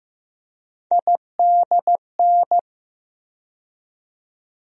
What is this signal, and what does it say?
Morse code, heard as a keyed tone, about 15 wpm, 700 Hz. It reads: IDN
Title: USA Call Sign Postfixes - 3 Letters 15wpm